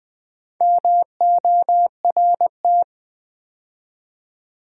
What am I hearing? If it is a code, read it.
Morse code, heard as a keyed tone, about 20 wpm, 700 Hz. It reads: MORT
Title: Build A Word - 4 Letters At A Time - 8-Letter Words - 20wpm